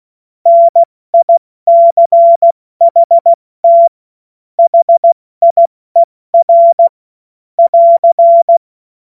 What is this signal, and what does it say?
Morse code, heard as a keyed tone, about 16 wpm, 680 Hz. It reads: NICHT HIER <AR>
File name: Morsekurs / Witze / Witz 07.November 2023 70 F80